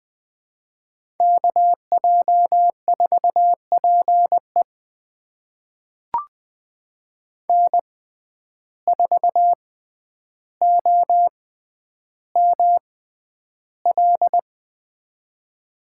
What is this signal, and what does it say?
Morse code, heard as a keyed tone, about 21 wpm, spaced as for 10 wpm, 700 Hz. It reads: KJ4PE N4OML
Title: USA Call Signs - 5 Characters - 6x Character Spacing - 20wpm